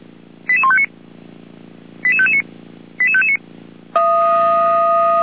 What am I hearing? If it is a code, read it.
Five-tone selective call (ZVEI), heard as quick runs of five tones, tones of about 70 ms. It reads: 88159 88559 88559